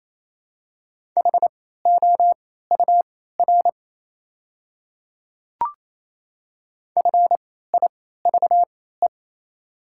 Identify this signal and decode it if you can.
Morse code, heard as a keyed tone, about 28 wpm, spaced as for 16 wpm, 700 Hz. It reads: HOUR FIVE
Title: 3x Character Spacing - Top 500 Words - 28wpm